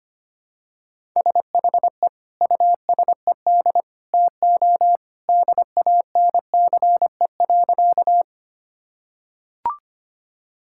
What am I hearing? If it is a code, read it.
Morse code, heard as a keyed tone, about 25 wpm, 700 Hz. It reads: SHE USED TO DANCE.
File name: practice / 25wpm / Sentences with 2-Word Phrases - 25wpm